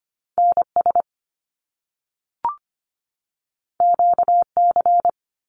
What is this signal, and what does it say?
Morse code, heard as a keyed tone, about 25 wpm, 700 Hz. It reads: NS QC
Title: Canadian Provinces and Territories 25wpm